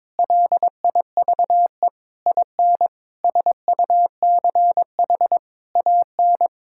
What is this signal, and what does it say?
Morse code, heard as a keyed tone, about 22 wpm, 700 Hz. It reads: LIVE IN SUCH AN
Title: Sentences from Top 1600 Words 22wpm